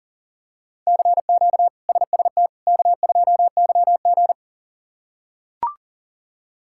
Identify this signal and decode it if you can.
Morse code, heard as a keyed tone, about 40 wpm, 700 Hz. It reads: CQ SST K2YG
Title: K1USN SST QSO Practice - 40wpm